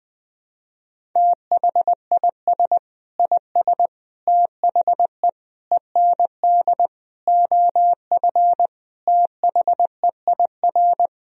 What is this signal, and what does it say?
Morse code, heard as a keyed tone, about 21 wpm, 700 Hz. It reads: THIS IS THE END OF THEIR
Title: Sentences from Top 1000 Words - Review Part 1 - 20wpm